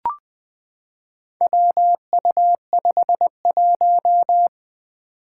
Morse code, heard as a keyed tone, about 20 wpm, 700 Hz. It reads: WU51
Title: Four Challenging Letter-Number Combinations 20wpm